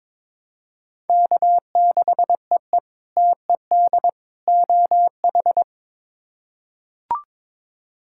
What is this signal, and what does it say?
Morse code, heard as a keyed tone, about 22 wpm, 700 Hz. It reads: K6EE TED OH
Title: K1USN SST QSO Practice - 22wpm